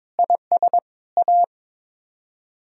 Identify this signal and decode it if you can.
Morse code, heard as a keyed tone, about 22 wpm, 700 Hz. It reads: IS A
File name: practice / 22wpm / Build A Phrase - 5-Word Phrases - 22wpm